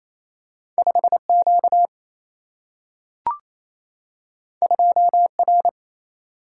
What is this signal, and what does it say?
Morse code, heard as a keyed tone, about 28 wpm, 700 Hz. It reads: 5 Q 2 R
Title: Single-Letter Single-Number 28wpm